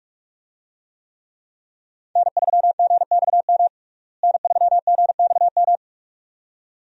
Morse code, heard as a keyed tone, about 46 wpm, 700 Hz. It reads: N3GXM N3GXM
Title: USA Call Signs - 5 Characters - Repeated 2x - 45wpm